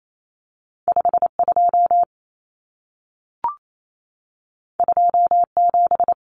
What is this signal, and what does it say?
Morse code, heard as a keyed tone, about 28 wpm, 700 Hz. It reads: 52 27
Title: Numbers - 2-Digit 28wpm